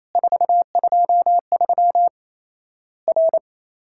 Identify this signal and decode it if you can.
Morse code, heard as a keyed tone, about 28 wpm, 700 Hz. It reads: 423 R
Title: Worked All Europe DX Contest - QTC Exchange - 28wpm